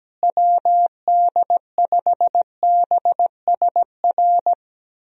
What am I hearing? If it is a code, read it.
Morse code, heard as a keyed tone, about 17 wpm, 700 Hz. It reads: WD5BSR
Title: USA Call Signs - 6 Characters - Repeated 3x - 17wpm